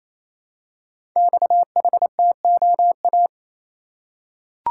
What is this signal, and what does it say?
Morse code, heard as a keyed tone, about 27 wpm, 700 Hz. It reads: XHTOA
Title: Code Groups - 5 Letters - 28wpm